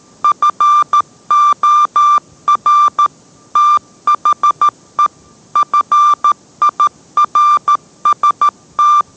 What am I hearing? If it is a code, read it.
Morse code, heard as a keyed tone, about 13 wpm, 1180 Hz. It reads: FOR THE FIRST